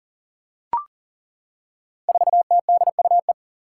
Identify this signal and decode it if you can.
Morse code, heard as a keyed tone, about 38 wpm, 700 Hz. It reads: 4TDUE
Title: Five Challenging Letter-Number Combinations 40wpm